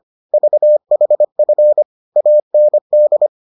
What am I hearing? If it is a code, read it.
Morse code, heard as a keyed tone, about 25 wpm, 600 Hz. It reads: VHF AND